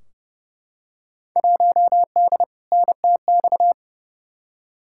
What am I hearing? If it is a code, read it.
Morse code, heard as a keyed tone, about 31 wpm, 700 Hz. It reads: 1D NTX
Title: Field Day Exchanges - 30wpm